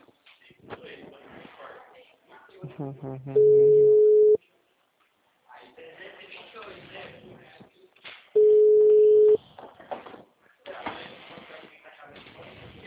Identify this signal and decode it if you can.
Ringback tone (Europe, 425 Hz), with speech.